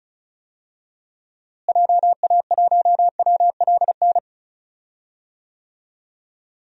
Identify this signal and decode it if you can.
Morse code, heard as a keyed tone, about 36 wpm, 700 Hz. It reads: JA1WLN
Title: DXCC-100 Call Signs - 6 Characters - Repeated 2x - 35wpm